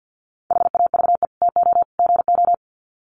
Morse code, heard as a keyed tone, about 50 wpm, 700 Hz. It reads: HAVE TO GO
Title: Sets of 3 Words 50wpm